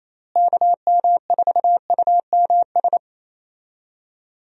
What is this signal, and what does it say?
Morse code, heard as a keyed tone, about 28 wpm, 700 Hz. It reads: KM4UMS
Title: USA Call Signs - 6 Characters - Repeated 2x - 28wpm